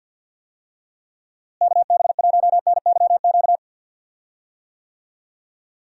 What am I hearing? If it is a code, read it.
Morse code, heard as a keyed tone, about 51 wpm, 700 Hz. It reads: KB1NYQ
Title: USA Call Signs - 6 Characters - Repeated 2x - 50wpm